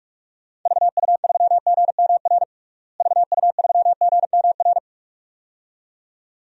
Morse code, heard as a keyed tone, about 46 wpm, 700 Hz. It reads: VU3GMR VU3GMR
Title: DXCC-100 Call Signs - 6 Characters - Repeated 2x - 45wpm